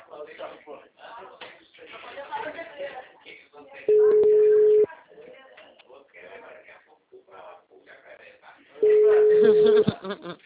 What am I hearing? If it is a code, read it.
Ringback tone (Europe, 425 Hz), with speech.